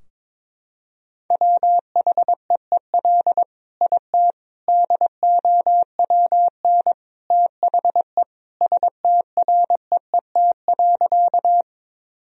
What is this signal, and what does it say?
Morse code, heard as a keyed tone, about 22 wpm, 700 Hz. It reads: WHEEL IT DOWN THE STREET.
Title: Sentences from Top 1000 Words - Review Part 3 - 22wpm